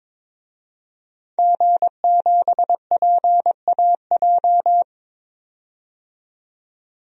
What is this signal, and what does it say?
Morse code, heard as a keyed tone, about 22 wpm, 700 Hz. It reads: G7PAJ
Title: DXCC-100 Call Signs - 5 Characters - 22wpm